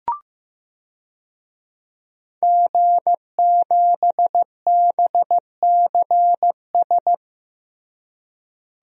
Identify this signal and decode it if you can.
Morse code, heard as a keyed tone, about 15 wpm, 700 Hz. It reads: G7BCS
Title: DXCC-100 Call Signs - 5 Characters - Repeated 3x - 15wpm